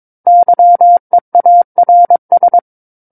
Morse code, heard as a keyed tone, about 22 wpm, 700 Hz. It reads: YEARS
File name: Book One / 13-Chapter 13 - 22 WPM